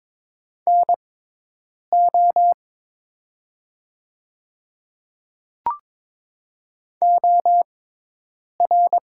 Morse code, heard as a keyed tone, about 22 wpm, spaced as for 8 wpm, 700 Hz. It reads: NO OR
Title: 6x Character Spacing - Top 200 Words - 22wpm